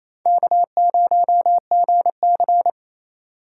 Morse code, heard as a keyed tone, about 38 wpm, spaced as for 33 wpm, 700 Hz. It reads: K0GC
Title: Speed Racing - USA 1x2 Call Signs 28wpm